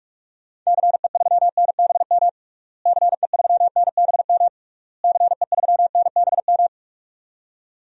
Morse code, heard as a keyed tone, about 41 wpm, 700 Hz. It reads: CE3NBM CE3NBM CE3NBM
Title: DXCC-100 Call Signs - 6 Characters - Repeated 3x - 45wpm